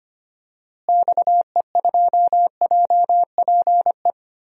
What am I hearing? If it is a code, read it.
Morse code, heard as a keyed tone, about 25 wpm, 700 Hz. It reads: XE2JPE
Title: DXCC-100 Call Signs - 6 Characters - Repeated 2x - 25wpm